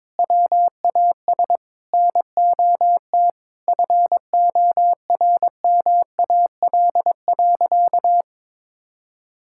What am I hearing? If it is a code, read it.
Morse code, heard as a keyed tone, about 22 wpm, 700 Hz. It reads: WAS NOT FORMAL.
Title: Sentences from Top 1600 Words 22wpm